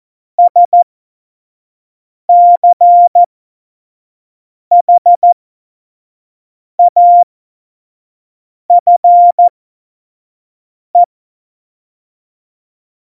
Morse code, heard as a keyed tone, about 14 wpm, spaced as for 5 wpm, 700 Hz. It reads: SCHAFE
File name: Morsekurs / Witze / Witz 04. Februar 2020, 25 F70